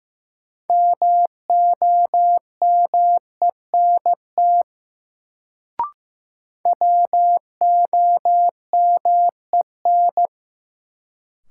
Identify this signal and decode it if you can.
Morse code, heard as a keyed tone, about 15 wpm, 700 Hz. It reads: MOMENT WOMEN
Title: Sets of 3 Words - New Word Drill - 15wpm